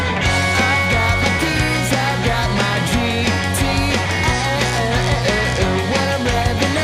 Dial-up modem handshake.